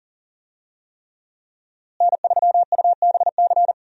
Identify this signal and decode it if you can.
Morse code, heard as a keyed tone, about 40 wpm, 700 Hz. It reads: N3UBC
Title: USA Call Signs - 5 Characters - Repeated 3x - 40wpm